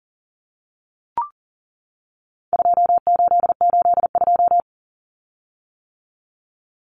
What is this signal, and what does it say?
Morse code, heard as a keyed tone, about 39 wpm, 700 Hz. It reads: 2882
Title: Common Cut Numbers - 4-Digit 40wpm